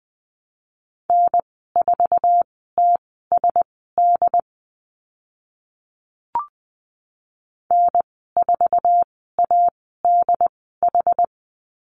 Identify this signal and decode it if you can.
Morse code, heard as a keyed tone, about 20 wpm, spaced as for 15 wpm, 700 Hz. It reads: N4TSD N4ADH
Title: 069 Lesson 12g - Letter d - Call Signs 20wpm